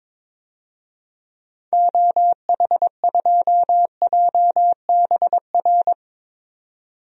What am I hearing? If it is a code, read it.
Morse code, heard as a keyed tone, about 22 wpm, 700 Hz. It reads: OH2JBR
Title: DXCC-100 Call Signs - 6 Characters - 22wpm